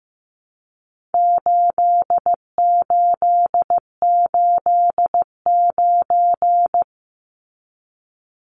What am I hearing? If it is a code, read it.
Morse code, heard as a keyed tone, about 15 wpm, 700 Hz. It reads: 8889
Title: Numbers - 4-Digit 15wpm